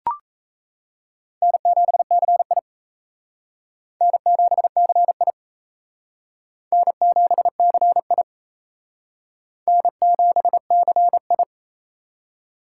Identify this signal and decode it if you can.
Morse code, heard as a keyed tone, about 34 wpm, 700 Hz. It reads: N7CI N7CI N7CI N7CI
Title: Speed Racing - USA 1x2 Call Signs 28wpm